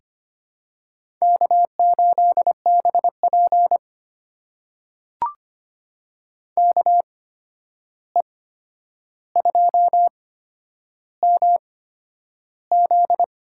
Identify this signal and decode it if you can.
Morse code, heard as a keyed tone, about 25 wpm, spaced as for 10 wpm, 700 Hz. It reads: K8BP KE2MZ
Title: USA Call Signs - 8x Character Spacing - 25wpm